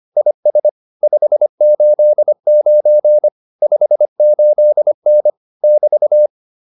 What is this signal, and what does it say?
Morse code, heard as a keyed tone, about 25 wpm, 600 Hz. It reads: IS 589 58N <BT>